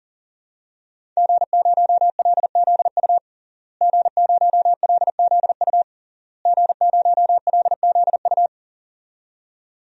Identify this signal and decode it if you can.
Morse code, heard as a keyed tone, about 40 wpm, 700 Hz. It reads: G0LZU G0LZU G0LZU
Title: DXCC-100 Call Signs - 5 Characters - Repeated 3x - 40wpm